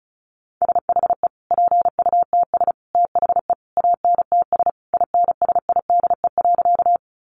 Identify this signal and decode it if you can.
Morse code, heard as a keyed tone, about 34 wpm, 700 Hz. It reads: SHE PUTS THE ANTS INSIDE.
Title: Easy Phrases 35wpm